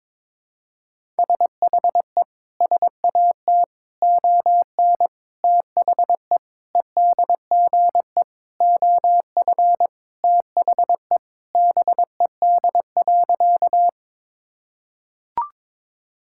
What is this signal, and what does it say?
Morse code, heard as a keyed tone, about 22 wpm, 700 Hz. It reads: SHE SAT ON THE EDGE OF THE BED.
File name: practice / 22wpm / Sentences with 4-Word Phrases - 22wpm